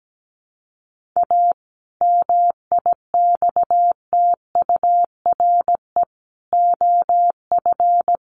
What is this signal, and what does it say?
Morse code, heard as a keyed tone, about 17 wpm, 700 Hz. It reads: A MIXTURE OF
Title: Sentences from Top 1500 Words 17wpm